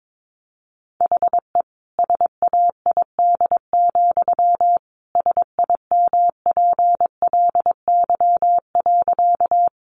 Morse code, heard as a keyed tone, about 22 wpm, 700 Hz. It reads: HE SAID, SIMPLY.
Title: Sentences from Top 600 Words 22wpm